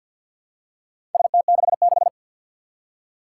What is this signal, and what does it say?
Morse code, heard as a keyed tone, about 46 wpm, 700 Hz. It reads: ST66
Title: Four Challenging Letter-Number Combinations 50wpm